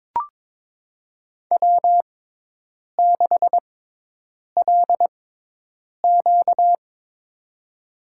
Morse code, heard as a keyed tone, about 22 wpm, spaced as for 8 wpm, 700 Hz. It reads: W6LQ
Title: USA Call Signs - 4 Characters - 6x Character Spacing - 22wpm